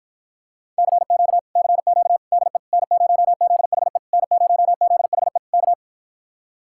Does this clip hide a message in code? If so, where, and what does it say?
Yes: Morse code from 0.8 to 5.7 s, heard as a keyed tone, about 52 wpm, 700 Hz. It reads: CQ CQ DE N0ZHE N0ZHE K